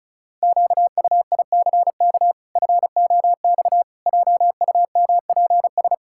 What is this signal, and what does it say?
Morse code, heard as a keyed tone, about 36 wpm, 700 Hz. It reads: QUICK FOX JUMPS
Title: Pangram Challenge v2 - 35wpm